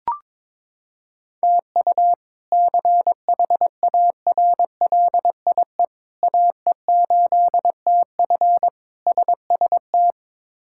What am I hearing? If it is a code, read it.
Morse code, heard as a keyed tone, about 22 wpm, 700 Hz. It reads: TU CHARLIE AE8TF SST
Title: K1USN SST QSO Practice - 22wpm